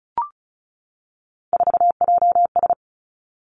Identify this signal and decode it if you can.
Morse code, heard as a keyed tone, about 34 wpm, 700 Hz. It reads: 4JS
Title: Three Challenging Letter-Number Combinations 35wpm